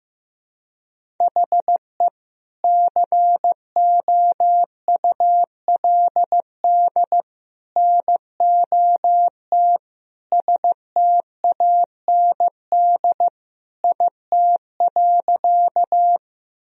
Morse code, heard as a keyed tone, about 15 wpm, 700 Hz. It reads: HE COULD NOT STAND IT.
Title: Sentences from Top 1000 Words - Review Part 1 - 15wpm